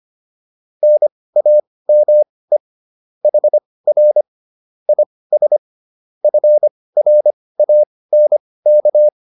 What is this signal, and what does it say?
Morse code, heard as a keyed tone, about 25 wpm, spaced as for 18 wpm, 600 Hz. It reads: NAME HR IS FRANK
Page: intermediate-practice-files